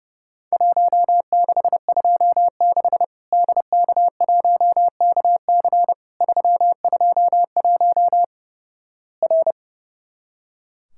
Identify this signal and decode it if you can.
Morse code, heard as a keyed tone, about 32 wpm, 700 Hz. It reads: 1626 DK1KC 321 R
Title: Worked All Europe DX Contest - QTC Exchange - 30wpm